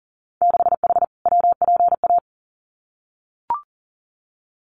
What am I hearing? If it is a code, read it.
Morse code, heard as a keyed tone, about 40 wpm, 700 Hz. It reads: 6H WPA
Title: Winter Field Day Exchanges Encore 40wpm